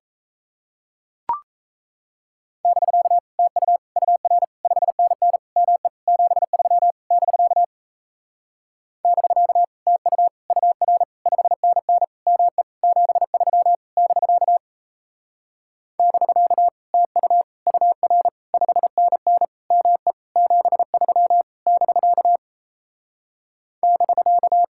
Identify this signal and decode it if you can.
Morse code, heard as a keyed tone, about 38 wpm, 700 Hz. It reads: <BK> TU UR 5NN ME 73 <BK> <BK> TU UR 5NN ME 73 <BK> <BK> TU UR 5NN ME 73 <BK> <BK>